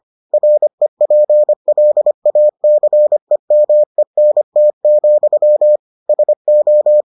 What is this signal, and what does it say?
Morse code, heard as a keyed tone, about 25 wpm, 600 Hz. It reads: REPLACEMENT, SO